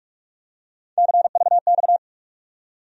Morse code, heard as a keyed tone, about 41 wpm, 700 Hz. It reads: CVX
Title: USA Call Sign Postfixes - 3 Letters 45wpm